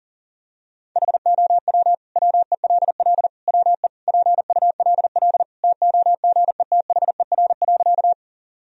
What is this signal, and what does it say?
Morse code, heard as a keyed tone, about 40 wpm, 700 Hz. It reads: HOW WELL WE PULL TOGETHER.
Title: Sentences from Top 1000 Words - Review Part 2 - 40wpm